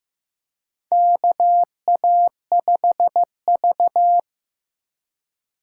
Morse code, heard as a keyed tone, about 15 wpm, 700 Hz. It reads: KA5V
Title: USA Call Signs - 3x Character Spacing - 15wpm